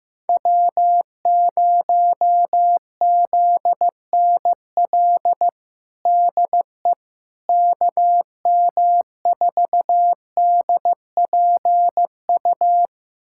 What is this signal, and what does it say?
Morse code, heard as a keyed tone, about 15 wpm, 700 Hz. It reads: W0ZNL DE KM4DPU
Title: USA Callsign DE Callsign 15wpm